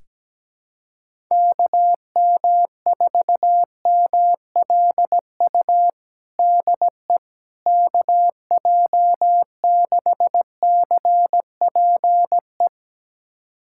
Morse code, heard as a keyed tone, about 17 wpm, 700 Hz. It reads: KM4MLU DE KJ6CPE
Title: USA Callsign DE Callsign 17wpm